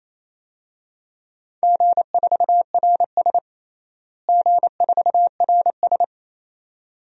Morse code, heard as a keyed tone, about 28 wpm, 700 Hz. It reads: G4RS G4RS
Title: DXCC-100 Call Signs - 4 Characters - Repeated 2x - 28wpm